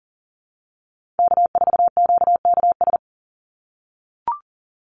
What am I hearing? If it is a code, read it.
Morse code, heard as a keyed tone, about 40 wpm, 700 Hz. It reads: K4QKS
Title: USA Call Signs 40wpm